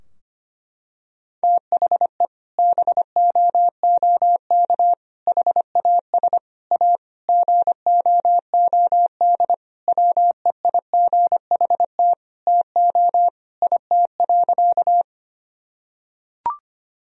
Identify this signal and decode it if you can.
Morse code, heard as a keyed tone, about 25 wpm, 700 Hz. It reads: THE BOOK HAS A GOOD WEIGHT TO IT.
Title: Sentences from Top 1000 Words - Review Part 3 - 25wpm